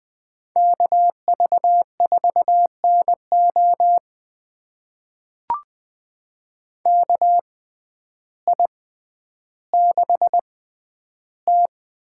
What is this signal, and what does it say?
Morse code, heard as a keyed tone, about 20 wpm, spaced as for 11 wpm, 700 Hz. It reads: KV4NO KI6T
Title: USA Call Signs - 5 Characters - 6x Character Spacing - 20wpm